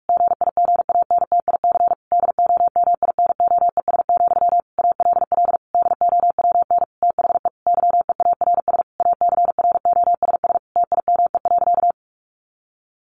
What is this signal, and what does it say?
Morse code, heard as a keyed tone, about 44 wpm, 700 Hz. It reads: GIGANTIC DOMINOES, ALL DOWN THE YEARS ACROSS TIME.
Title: A Sound Of Thunder 45wpm